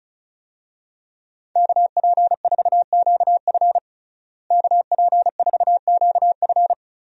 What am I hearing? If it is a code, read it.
Morse code, heard as a keyed tone, about 36 wpm, 700 Hz. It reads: KP4QF KP4QF
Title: DXCC-100 Call Signs - 5 Characters - Repeated 2x - 35wpm